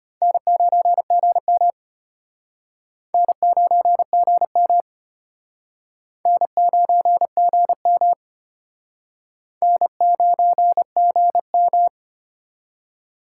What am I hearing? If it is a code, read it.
Morse code, heard as a keyed tone, about 32 wpm, 700 Hz. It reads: N9GM N9GM N9GM N9GM